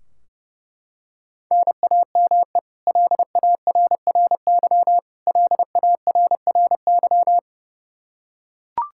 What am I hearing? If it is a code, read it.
Morse code, heard as a keyed tone, about 32 wpm, 700 Hz. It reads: NAME LARRY LARRY
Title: Practice QSO - Exchange 1 - Beginning - 30wpm